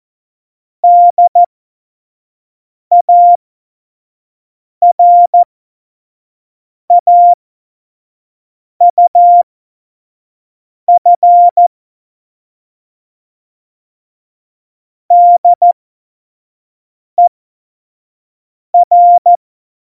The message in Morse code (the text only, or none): DARAUF DER